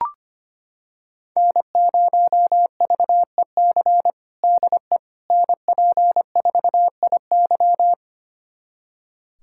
Morse code, heard as a keyed tone, about 25 wpm, 700 Hz. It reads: N0VEC DE NP4IY
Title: USA Callsign DE Callsign - 5 Characters - 25wpm